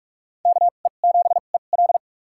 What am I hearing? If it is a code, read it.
Morse code, heard as a keyed tone, about 41 wpm, spaced as for 31 wpm, 700 Hz. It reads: KE7EL
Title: USA Call Signs - 5 Characters - 2x Character Spacing - 45wpm